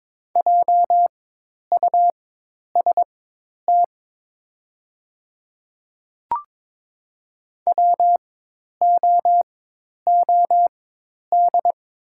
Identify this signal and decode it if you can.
Morse code, heard as a keyed tone, about 22 wpm, spaced as for 10 wpm, 700 Hz. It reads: JUST WOOD
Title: 4x Character Spacing - Top 300 Words - 22wpm